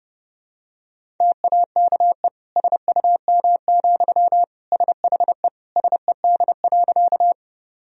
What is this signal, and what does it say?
Morse code, heard as a keyed tone, about 30 wpm, 700 Hz. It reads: TAKE SUM, SHE SED.
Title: Sentences from Top 200 Words and Abbreviations 30wpm